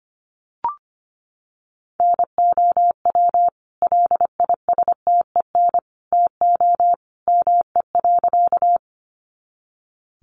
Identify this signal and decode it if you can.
Morse code, heard as a keyed tone, about 25 wpm, 700 Hz. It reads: NOW LISTEN TO ME.
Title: Sentences from Top 400 Words 25wpm